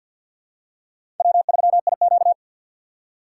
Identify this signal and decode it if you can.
Morse code, heard as a keyed tone, about 51 wpm, 700 Hz. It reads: W3IQ
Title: USA Call Signs - 4 Characters - 3x Character Spacing - 50wpm